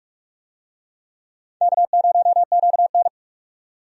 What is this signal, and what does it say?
Morse code, heard as a keyed tone, about 46 wpm, 700 Hz. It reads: K0QN
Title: USA Call Signs - 4 Characters - 45wpm